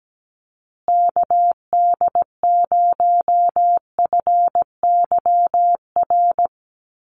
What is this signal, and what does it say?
Morse code, heard as a keyed tone, about 17 wpm, 700 Hz. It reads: KD0FYR
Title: USA Call Signs Repeated 2x 17wpm